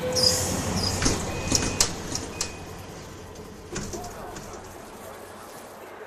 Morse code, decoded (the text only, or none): none